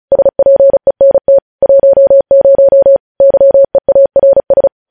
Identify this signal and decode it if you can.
Morse code, heard as a keyed tone, about 36 wpm, 550 Hz. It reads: SPENT 10 YEARS
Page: news headlines in morse code at 35 wpm